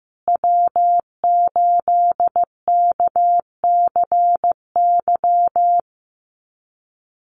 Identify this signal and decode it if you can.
Morse code, heard as a keyed tone, about 15 wpm, 700 Hz. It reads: W8KCY
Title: USA Call Signs Repeated 2x 15wpm